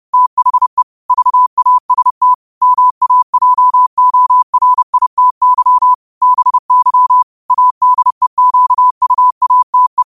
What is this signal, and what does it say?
Morse code, heard as a keyed tone, about 30 wpm, 1000 Hz. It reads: THE VAST MAJORITY BY ADEQUATE
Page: New quotes every day in morse code at 30 Words per minute.